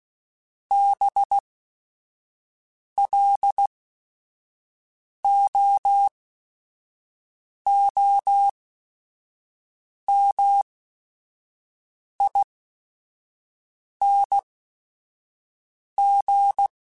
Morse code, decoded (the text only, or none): BLOOMING